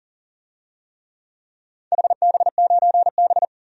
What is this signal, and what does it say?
Morse code, heard as a keyed tone, about 40 wpm, 700 Hz. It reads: HB9B
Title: DXCC-100 Call Signs - 4 Characters - 40wpm